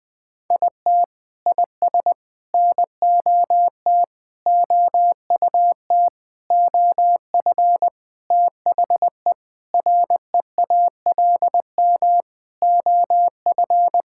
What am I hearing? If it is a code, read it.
Morse code, heard as a keyed tone, about 20 wpm, 700 Hz. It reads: IT IS NOT OUT OF THE REALM OF
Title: Sentences with 9-Word Phrases - 20wpm